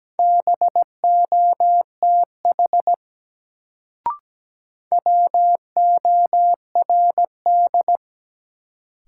Morse code, heard as a keyed tone, about 17 wpm, 700 Hz. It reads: BOTH WORD
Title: Sentences from Top 300 Words - New Word Drill - 17wpm